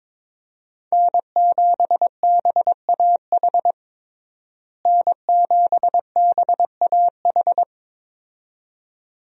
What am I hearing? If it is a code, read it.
Morse code, heard as a keyed tone, about 22 wpm, 700 Hz. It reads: N7BAH N7BAH
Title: USA Call Signs - 5 Characters - Repeated 2x - 22wpm